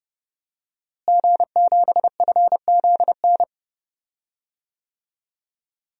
Morse code, heard as a keyed tone, about 30 wpm, 700 Hz. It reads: G7FZN
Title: DXCC-100 Call Signs - 5 Characters - Repeated 2x - 30wpm